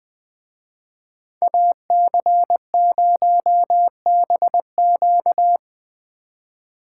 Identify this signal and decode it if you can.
Morse code, heard as a keyed tone, about 23 wpm, 700 Hz. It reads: AC0BQ
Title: Speed Racing - USA Call Signs 20wpm